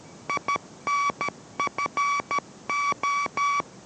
Morse code, heard as a keyed tone, about 13 wpm, 1140 Hz. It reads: INFO